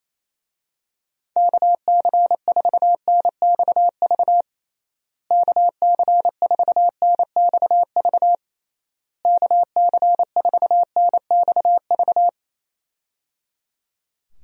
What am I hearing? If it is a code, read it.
Morse code, heard as a keyed tone, about 27 wpm, 700 Hz. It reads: KC4NXV KC4NXV KC4NXV
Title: USA Call Signs - 6 Characters - Repeated 3x - 28wpm